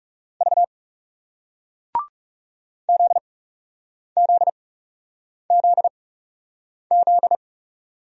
Morse code, heard as a keyed tone, about 41 wpm, spaced as for 8 wpm, 700 Hz. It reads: V ZZZZ